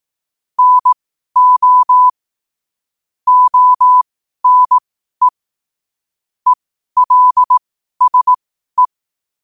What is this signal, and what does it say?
Morse code, heard as a keyed tone, about 18 wpm, spaced as for 12 wpm, 1000 Hz. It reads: NO ONE ELSE
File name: qotdfiles / 12wpm